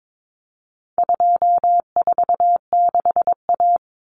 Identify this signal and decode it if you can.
Morse code, heard as a keyed tone, about 22 wpm, 700 Hz. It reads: 246A
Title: Common Cut Numbers - 4-Digit 22wpm